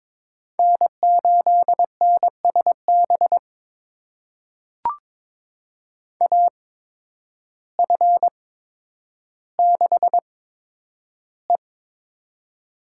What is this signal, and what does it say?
Morse code, heard as a keyed tone, about 22 wpm, spaced as for 10 wpm, 700 Hz. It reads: N8NSB AF6E
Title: USA Call Signs - 5 Characters - 8x Character Spacing - 22wpm